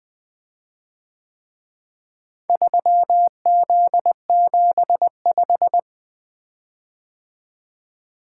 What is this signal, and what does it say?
Morse code, heard as a keyed tone, about 22 wpm, 700 Hz. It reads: 3Z75
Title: Speed-Racing - Four Letter-Number Combinations - 20wpm